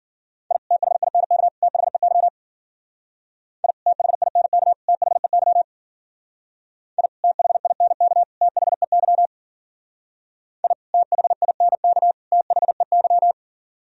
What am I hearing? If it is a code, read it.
Morse code, heard as a keyed tone, about 50 wpm, 700 Hz. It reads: I THINK THEY I THINK THEY I THINK THEY I THINK THEY
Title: Speed-Racing - Sets of 3 Words 40wpm